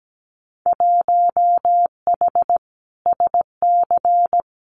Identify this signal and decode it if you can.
Morse code, heard as a keyed tone, about 17 wpm, 690 Hz. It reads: 1H SC
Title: Winter Field Day Exchanges 17wpm